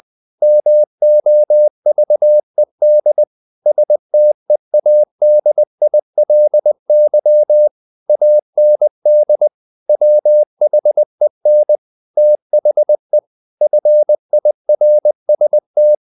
Morse code, heard as a keyed tone, about 20 wpm, 600 Hz. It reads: MOVED STEADILY AND WHEN THE FIRST
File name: the old man and the sea / the old man and the sea part 1 20wpm